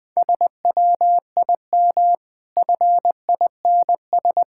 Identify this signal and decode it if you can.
Morse code, heard as a keyed tone, about 22 wpm, 700 Hz. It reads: SWIM FINS
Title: Easy Words - Speed-Racing with Hints - 20wpm